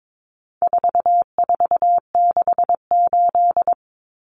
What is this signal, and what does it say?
Morse code, heard as a keyed tone, about 22 wpm, 700 Hz. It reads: 4468
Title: Numbers - 4-Digit 22wpm